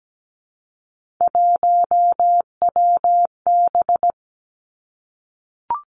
Morse code, heard as a keyed tone, about 17 wpm, 700 Hz. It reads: 1WB
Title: Three Challenging Letter-Number Combinations 17wpm